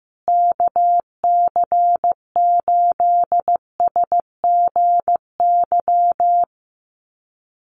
Morse code, heard as a keyed tone, about 14 wpm, 700 Hz. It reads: KC8SGY
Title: USA Call Signs Repeated 2x 15wpm